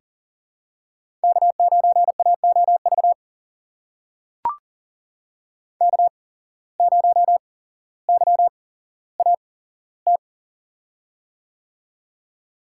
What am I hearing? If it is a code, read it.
Morse code, heard as a keyed tone, about 39 wpm, spaced as for 17 wpm, 700 Hz. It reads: K9AOV K0YAT
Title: USA Call Signs - 5 Characters - 8x Character Spacing - 40wpm